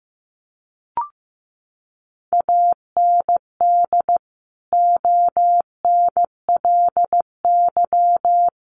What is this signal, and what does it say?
Morse code, heard as a keyed tone, about 15 wpm, 700 Hz. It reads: AND ONLY
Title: Sentences from Top 1400 Words 15wpm